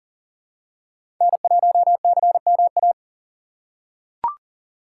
Morse code, heard as a keyed tone, about 37 wpm, 700 Hz. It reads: N1CMA
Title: USA Call Signs - 5 Characters - 6x Character Spacing - 40wpm